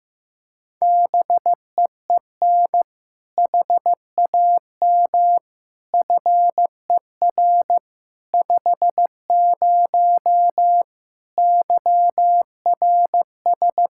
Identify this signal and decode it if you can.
Morse code, heard as a keyed tone, about 15 wpm, 700 Hz. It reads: BEEN HAM FER 50 YRS
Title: Practice QSO - Exchange 3 - More Info - 15wpm